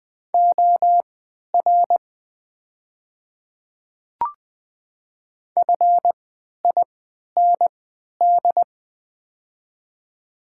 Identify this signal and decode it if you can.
Morse code, heard as a keyed tone, about 20 wpm, spaced as for 11 wpm, 700 Hz. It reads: OR FIND
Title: 3x Character Spacing - Top 500 Words - 20wpm